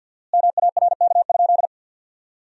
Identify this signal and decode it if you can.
Morse code, heard as a keyed tone, about 50 wpm, 700 Hz. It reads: MARK?
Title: 4x Word-Spacing - Sentences from Top 300 Words - 50wpm